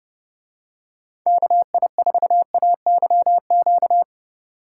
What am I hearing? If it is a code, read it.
Morse code, heard as a keyed tone, about 28 wpm, 700 Hz. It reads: KI4AYQ